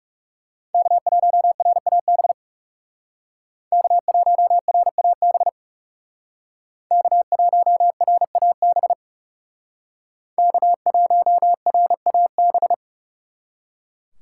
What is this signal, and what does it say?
Morse code, heard as a keyed tone, about 38 wpm, 700 Hz. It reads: K1RAB K1RAB K1RAB K1RAB